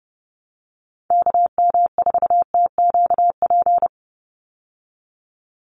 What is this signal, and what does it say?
Morse code, heard as a keyed tone, about 30 wpm, 700 Hz. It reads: KM4TQP
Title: USA Call Signs 30wpm